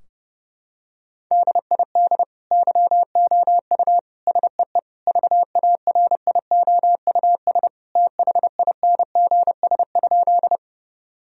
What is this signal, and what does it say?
Morse code, heard as a keyed tone, about 32 wpm, 700 Hz. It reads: DID YOU SEE VARIOUS THINGS?
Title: Sentences from Top 1000 Words - Review Part 4 - 30wpm